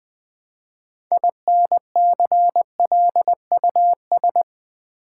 Morse code, heard as a keyed tone, about 20 wpm, 700 Hz. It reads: INCLUS
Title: Build A Word - 3 Letters At A Time - 9-Letter Words - 20wpm